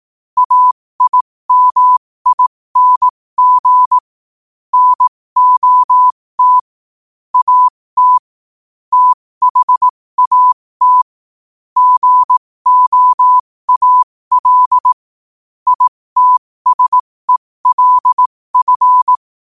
Morse code, heard as a keyed tone, about 18 wpm, spaced as for 15 wpm, 1000 Hz. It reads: AIMING NOT AT THAT GOAL ITSELF
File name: qotdfiles / 15wpm